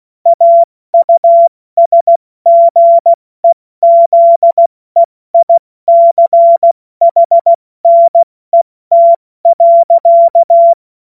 Morse code, heard as a keyed tone, about 16 wpm, spaced as for 14 wpm, 680 Hz. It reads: AUSGEZEICHNET.
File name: Morsekurs / Witze / Witz 06. Dezember 2022 70 F80